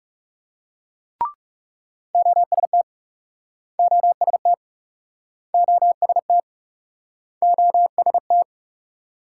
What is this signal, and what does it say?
Morse code, heard as a keyed tone, about 43 wpm, 700 Hz. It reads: OST OST OST OST